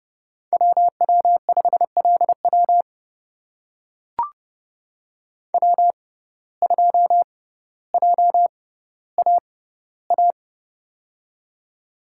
Morse code, heard as a keyed tone, about 32 wpm, spaced as for 16 wpm, 700 Hz. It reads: WW5LW W2JAA
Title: USA Call Signs - 5 Characters - 6x Character Spacing - 30wpm